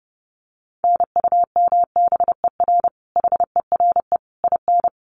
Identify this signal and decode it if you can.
Morse code, heard as a keyed tone, about 30 wpm, 700 Hz. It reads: NUMBER HERE IN
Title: Sets of 5 Words Encore2 30wpm